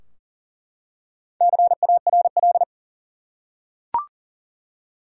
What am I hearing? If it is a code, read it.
Morse code, heard as a keyed tone, about 40 wpm, 700 Hz. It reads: CARL
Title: USA Common Names - Max 5 Letters - 40wpm